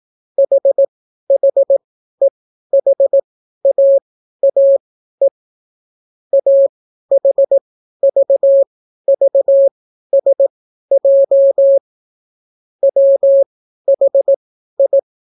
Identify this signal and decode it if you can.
Morse code, heard as a keyed tone, about 18 wpm, spaced as for 12 wpm, 550 Hz. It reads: HHEHAAE AHVVSJ WHI